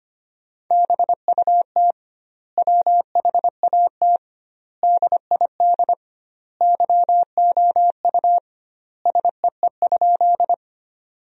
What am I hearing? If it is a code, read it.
Morse code, heard as a keyed tone, about 25 wpm, 700 Hz. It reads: BUT WHAT DID YOU SEE?
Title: 2x Word-Spacing - Sentences from Top 100 Words - 25wpm